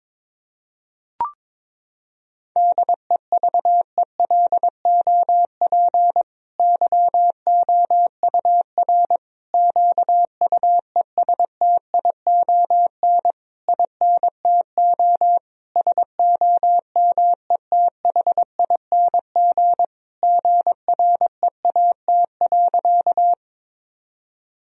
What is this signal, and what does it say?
Morse code, heard as a keyed tone, about 22 wpm, 700 Hz. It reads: DEVELOP YOUR QUESTION INTO SOMETHING GREAT.
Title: Sentences from Top 1000 Words - Review Part 2 - 22wpm